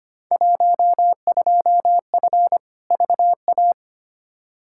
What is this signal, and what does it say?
Morse code, heard as a keyed tone, about 25 wpm, 700 Hz. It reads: 12F VA
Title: Field Day Exchanges Encore - 25wpm